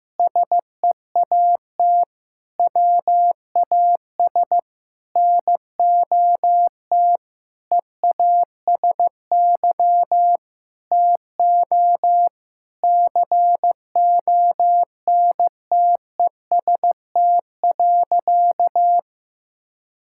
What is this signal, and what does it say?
Morse code, heard as a keyed tone, about 15 wpm, 700 Hz. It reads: SEAT WAS NOT EASY TO CONTEST.
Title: Sentences from Top 1600 Words 15wpm